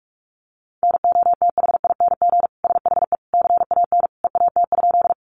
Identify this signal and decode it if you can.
Morse code, heard as a keyed tone, about 44 wpm, 700 Hz. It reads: NOTHING SHE CAN EAT?
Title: Sentences from Top 500 Words 45wpm